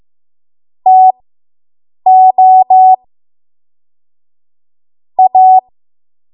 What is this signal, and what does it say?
Morse code, heard as a keyed tone, about 15 wpm, spaced as for 7 wpm, 750 Hz. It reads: TO A